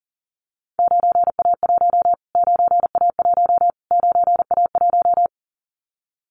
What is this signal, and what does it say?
Morse code, heard as a keyed tone, about 40 wpm, 700 Hz. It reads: 9A1 9A1 9A1
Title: Callsign Prefixes for Top 25 DXCC - Repeated 3x - 40wpm